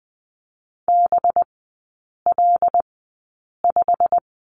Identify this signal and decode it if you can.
Morse code, heard as a keyed tone, about 20 wpm, spaced as for 8 wpm, 700 Hz. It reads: BL5
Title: 184 Lesson 30e - Letter b - Three Characters 20wpm